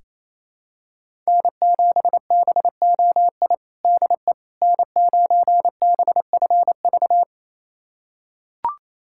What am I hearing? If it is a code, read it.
Morse code, heard as a keyed tone, about 28 wpm, 700 Hz. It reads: N7BOI DE N9BFV
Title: USA Callsign DE Callsign - 5 Characters - 28wpm